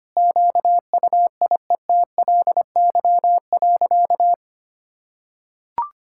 Morse code, heard as a keyed tone, about 25 wpm, 700 Hz. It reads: QUIETLY.
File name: practice / 25wpm / Sentences with 2-Word Phrases - 25wpm